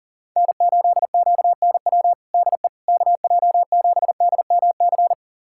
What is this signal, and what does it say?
Morse code, heard as a keyed tone, about 40 wpm, 700 Hz. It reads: N8QNW DE KJ7DMC
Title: USA Callsign DE Callsign 40wpm